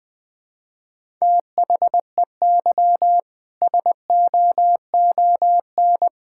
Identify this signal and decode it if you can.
Morse code, heard as a keyed tone, about 21 wpm, 700 Hz. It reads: THEY SOON
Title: Sentences from Top 1000 Words - Review Part 5 - 20wpm